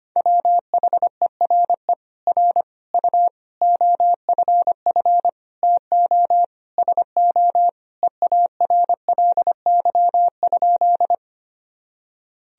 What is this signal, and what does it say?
Morse code, heard as a keyed tone, about 25 wpm, 700 Hz. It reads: WHERE R U OFF TO SO EARLY?
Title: Sentences from Top 400 Words and Abbreviations 25wpm